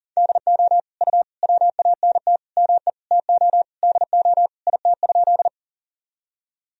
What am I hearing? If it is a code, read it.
Morse code, heard as a keyed tone, about 40 wpm, 700 Hz. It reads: DO U WANT ME TO DO IT?
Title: Sentences from Top 200 Words and Abbreviations 40wpm